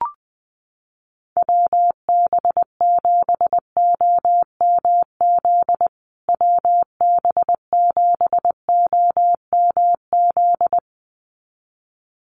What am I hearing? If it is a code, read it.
Morse code, heard as a keyed tone, about 20 wpm, 700 Hz. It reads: WB7OMZ WB7OMZ
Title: USA Call Signs Repeated 2x 20wpm